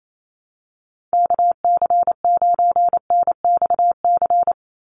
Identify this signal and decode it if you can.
Morse code, heard as a keyed tone, about 28 wpm, 700 Hz. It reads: KC9NXC
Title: USA Call Signs 28wpm